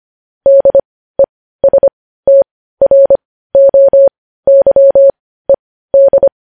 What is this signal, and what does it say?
Morse code, heard as a keyed tone, about 25 wpm, spaced as for 15 wpm, 550 Hz. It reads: DESTROYED
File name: podcast / News headlines-1x-25x15